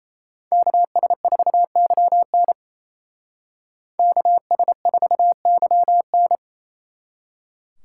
Morse code, heard as a keyed tone, about 33 wpm, 700 Hz. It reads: KS4YN KS4YN
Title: Speed Racing - USA 2x2 Call Signs 28wpm